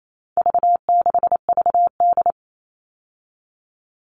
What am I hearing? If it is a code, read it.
Morse code, heard as a keyed tone, about 28 wpm, 700 Hz. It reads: V6VD
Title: All Cut Numbers - 4-Digit 28wpm